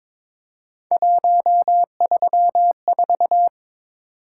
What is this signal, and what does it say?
Morse code, heard as a keyed tone, about 23 wpm, 700 Hz. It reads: 134
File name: practice / 22wpm / Three Letter-Number Combinations - 22wpm